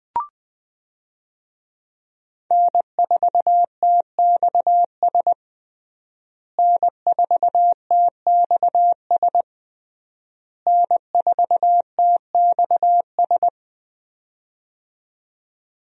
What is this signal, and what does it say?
Morse code, heard as a keyed tone, about 20 wpm, 700 Hz. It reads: N4TXS N4TXS N4TXS
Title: USA Call Signs - 5 Characters - Repeated 3x - 20wpm